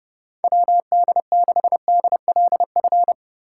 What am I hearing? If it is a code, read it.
Morse code, heard as a keyed tone, about 30 wpm, 700 Hz. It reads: WD6DLF
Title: USA Call Signs - 4x Character Spacing - 30wpm